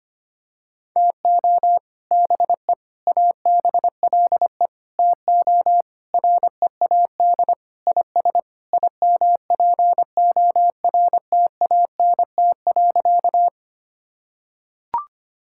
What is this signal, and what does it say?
Morse code, heard as a keyed tone, about 25 wpm, 700 Hz. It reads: TO BE ABLE TO READ IS IMPORTANT.
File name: practice / 25wpm / Sentences with 3-Word Phrases - 25wpm